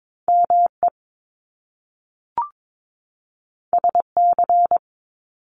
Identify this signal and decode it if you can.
Morse code, heard as a keyed tone, about 22 wpm, 700 Hz. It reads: ME SC
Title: Top Two Letter Combinations in Words 22wpm